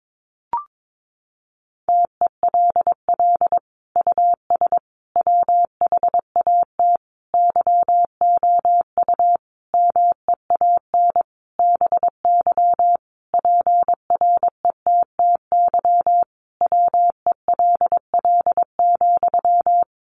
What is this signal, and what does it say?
Morse code, heard as a keyed tone, about 22 wpm, 700 Hz. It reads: TELL US WHAT YOU MEAN BY PRETTY WELL,
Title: Sentences from Top 1000 Words 22wpm